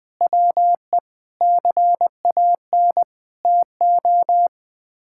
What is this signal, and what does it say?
Morse code, heard as a keyed tone, about 20 wpm, 700 Hz. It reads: WE CAN TO
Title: Sets of 8 Words - 20wpm